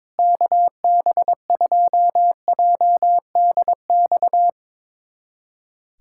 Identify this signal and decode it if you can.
Morse code, heard as a keyed tone, about 22 wpm, 700 Hz. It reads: KB2JDX
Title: USA Callsign DE Callsign 22wpm